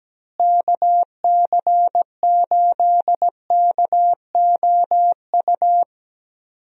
Morse code, heard as a keyed tone, about 17 wpm, 700 Hz. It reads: KC8KOU
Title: USA Call Signs - 6 Characters - Repeated 3x - 17wpm